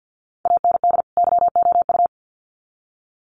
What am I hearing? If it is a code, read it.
Morse code, heard as a keyed tone, about 50 wpm, 700 Hz. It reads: AND YOU
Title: Sets of 2 Words 50wpm